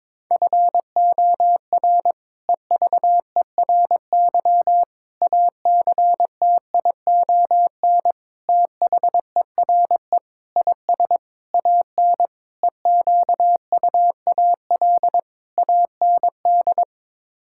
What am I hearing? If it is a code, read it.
Morse code, heard as a keyed tone, about 22 wpm, 700 Hz. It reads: FOR EVERY ACTION THERE IS AN EQUAL AND
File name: practice / 22wpm / Sets of 8 Words - 22wpm